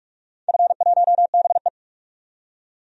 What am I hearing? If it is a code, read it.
Morse code, heard as a keyed tone, about 41 wpm, 700 Hz. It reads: F1BE
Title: DXCC-100 Call Signs - 4 Characters - 45wpm